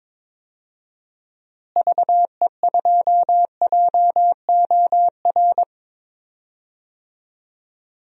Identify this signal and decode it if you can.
Morse code, heard as a keyed tone, about 21 wpm, 700 Hz. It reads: VE2JOR
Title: DXCC-100 Call Signs - 6 Characters - Repeated 2x - 22wpm